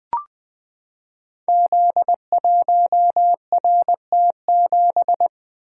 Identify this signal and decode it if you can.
Morse code, heard as a keyed tone, about 20 wpm, 700 Hz. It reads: Z1RT7
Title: Code Groups - 5 Letter:Numbers - 20wpm